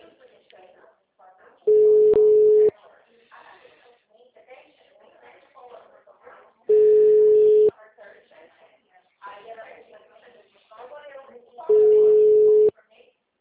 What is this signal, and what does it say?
Ringback tone (Europe, 425 Hz), with speech.
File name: out- -2026- - - .15586